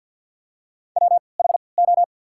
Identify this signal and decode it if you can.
Morse code, heard as a keyed tone, about 54 wpm, spaced as for 29 wpm, 690 Hz. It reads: WHO
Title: 3x Character Spacing - Top 100 Words - 50wpm